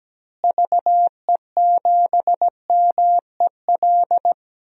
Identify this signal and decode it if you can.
Morse code, heard as a keyed tone, about 17 wpm, 700 Hz. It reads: VE7MEL
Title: DXCC-100 Call Signs - 6 Characters - Repeated 3x - 17wpm